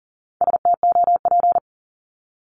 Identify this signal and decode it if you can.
Morse code, heard as a keyed tone, about 39 wpm, 700 Hz. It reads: STOP
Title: Multi-letter Phonemes 40wpm